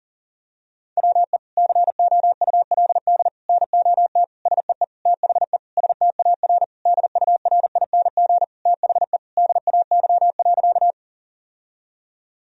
Morse code, heard as a keyed tone, about 40 wpm, 700 Hz. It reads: WE COULD NOT SEE THE STAR DURING THE DAY.
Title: Sentences from Top 1000 Words - Review Part 2 - 40wpm